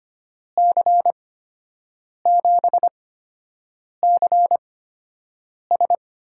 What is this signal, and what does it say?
Morse code, heard as a keyed tone, about 25 wpm, spaced as for 7 wpm, 700 Hz. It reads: C7CS
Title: USA Call Signs - 8x Character Spacing - 25wpm